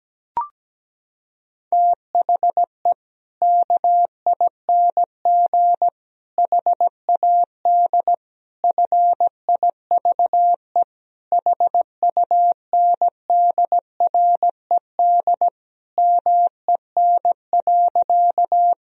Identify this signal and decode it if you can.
Morse code, heard as a keyed tone, about 17 wpm, 700 Hz. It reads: THE KING HAD FIVE HUNDRED MEN.
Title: Sentences from Top 1000 Words - Review Part 2 - 17wpm